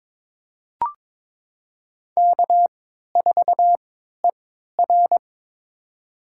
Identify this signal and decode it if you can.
Morse code, heard as a keyed tone, about 22 wpm, spaced as for 12 wpm, 700 Hz. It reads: K4ER
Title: USA Call Signs - 3x Character Spacing - 22wpm